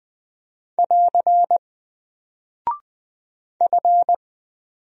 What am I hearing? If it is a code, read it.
Morse code, heard as a keyed tone, about 20 wpm, 700 Hz. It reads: <AR> F
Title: Single Character (Letter:Number:Punctuation:Prosign) 20wpm